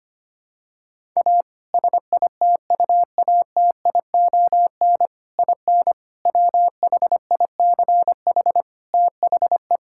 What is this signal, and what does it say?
Morse code, heard as a keyed tone, about 25 wpm, 700 Hz. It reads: A SITUATION IN WHICH THE
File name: practice / 25wpm / Sentences with 3-Word Phrases - 25wpm